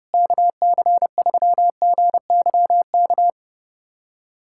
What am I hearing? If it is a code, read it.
Morse code, heard as a keyed tone, about 30 wpm, 700 Hz. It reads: KC3GYK
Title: USA Callsign DE Callsign 30wpm